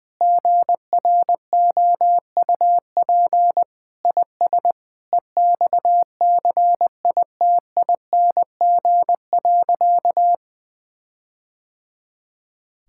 Morse code, heard as a keyed tone, about 20 wpm, 700 Hz. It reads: GROUP IS EXCITING.
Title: Sentences with 4-Word Phrases - Spoken Beginning - 20wpm